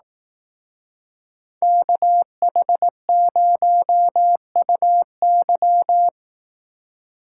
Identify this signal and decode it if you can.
Morse code, heard as a keyed tone, about 18 wpm, 700 Hz. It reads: KH0UY
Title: Speed Racing - USA 2x2 Call Signs 15wpm